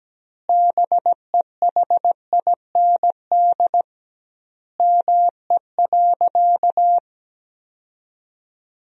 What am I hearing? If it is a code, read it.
Morse code, heard as a keyed tone, about 17 wpm, 700 Hz. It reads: BEHIND ME.
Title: 2x Word-Spacing - Sentences from Top 500 Words - 17wpm